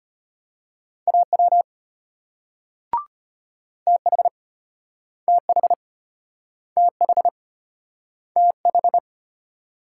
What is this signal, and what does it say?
Morse code, heard as a keyed tone, about 34 wpm, 700 Hz. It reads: AW TH TH TH TH